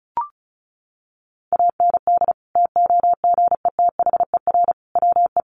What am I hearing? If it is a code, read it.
Morse code, heard as a keyed tone, about 34 wpm, 700 Hz. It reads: AND TOGETHER WE
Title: Sentences from Top 600 Words 35wpm